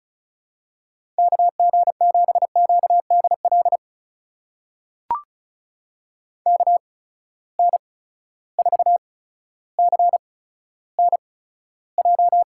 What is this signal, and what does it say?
Morse code, heard as a keyed tone, about 36 wpm, spaced as for 15 wpm, 700 Hz. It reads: KG7QDL KN4CNJ
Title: USA Call Signs - 6 Characters - 8x Character Spacing - 35wpm